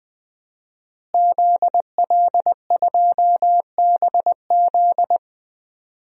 Morse code, heard as a keyed tone, about 20 wpm, 700 Hz. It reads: ZL2BZ
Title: DXCC-100 Call Signs - 5 Characters - Repeated 3x - 20wpm